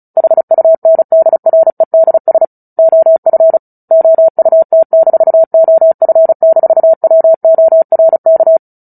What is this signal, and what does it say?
Morse code, heard as a keyed tone, about 34 wpm, 650 Hz. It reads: HUNDREDS OF OUT-OF-WORK
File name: Book Two / 09-Chapter 9 - 35 WPM